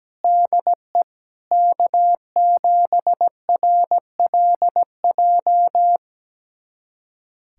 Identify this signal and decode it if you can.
Morse code, heard as a keyed tone, about 17 wpm, 700 Hz. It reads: DE K7RLJ
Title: USA Callsign DE Callsign - 5 Characters - 17wpm